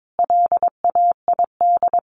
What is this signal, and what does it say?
Morse code, heard as a keyed tone, about 22 wpm, 700 Hz. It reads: LAID
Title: A Sound Of Thunder 22wpm